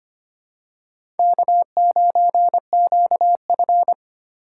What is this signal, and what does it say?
Morse code, heard as a keyed tone, about 25 wpm, 700 Hz. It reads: K9QF
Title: USA Call Signs - 4 Characters - 3x Character Spacing - 25wpm